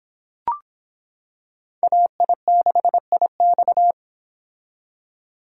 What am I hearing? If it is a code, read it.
Morse code, heard as a keyed tone, about 25 wpm, 700 Hz. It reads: AI6IX